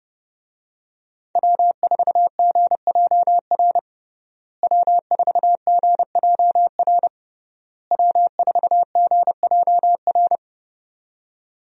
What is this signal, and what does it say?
Morse code, heard as a keyed tone, about 30 wpm, 700 Hz. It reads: W4GJR W4GJR W4GJR
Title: USA Call Signs - 5 Characters - Repeated 3x - 30wpm